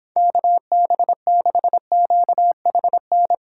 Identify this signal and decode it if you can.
Morse code, heard as a keyed tone, about 29 wpm, 700 Hz. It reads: KB6QHN